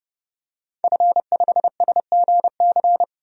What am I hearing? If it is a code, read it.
Morse code, heard as a keyed tone, about 30 wpm, 700 Hz. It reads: F5SGC
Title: DXCC-100 Call Signs - 5 Characters - 30wpm